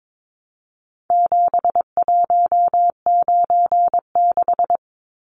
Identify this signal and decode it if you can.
Morse code, heard as a keyed tone, about 22 wpm, 700 Hz. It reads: 7196
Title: Numbers - 4-Digit 22wpm